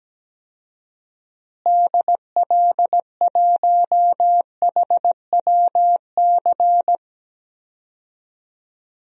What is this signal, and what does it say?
Morse code, heard as a keyed tone, about 17 wpm, 700 Hz. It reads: DL1HWC
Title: DXCC-100 Call Signs - 6 Characters - Repeated 3x - 17wpm